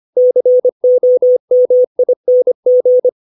Morse code, heard as a keyed tone, about 25 wpm, 500 Hz. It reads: COMING
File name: 632 words / 632 Words 25WPM